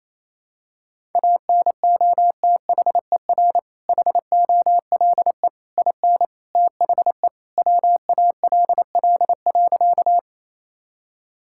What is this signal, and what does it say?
Morse code, heard as a keyed tone, about 28 wpm, 700 Hz. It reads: ANOTHER HOLE IN THE WALL.
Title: Sentences from Top 1600 Words 28wpm